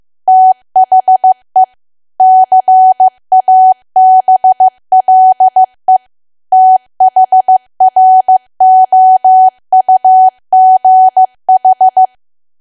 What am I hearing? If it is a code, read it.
Morse code, heard as a keyed tone, about 15 wpm, 750 Hz. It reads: THE CABLE THROUGH
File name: cwa / W1AW-Level2 / 15WPM